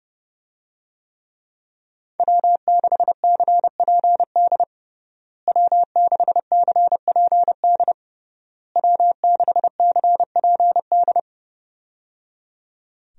Morse code, heard as a keyed tone, about 29 wpm, 700 Hz. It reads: W6CPD W6CPD W6CPD
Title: USA Call Signs - 5 Characters - Repeated 3x - 30wpm